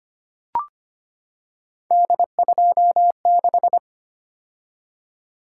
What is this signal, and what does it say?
Morse code, heard as a keyed tone, about 25 wpm, 700 Hz. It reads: D26
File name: Three Letter-Number Combinations - 25wpm